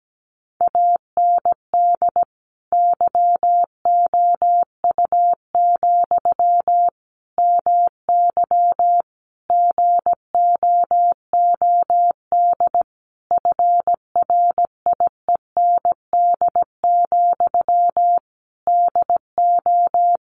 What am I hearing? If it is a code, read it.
Morse code, heard as a keyed tone, about 17 wpm, 700 Hz. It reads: AND YOU, MY GOOD FRIEND, DO
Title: Sentences from Top 1100 Words 17wpm